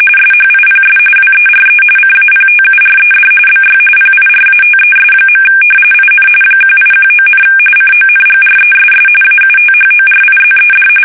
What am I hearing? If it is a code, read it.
RTTY, about 45.45 baud. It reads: SHIPS WERE CAPTURED BY PIRATES. IN 1819, CONGRESS AUTHORIZED
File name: fsk850 / usnavy2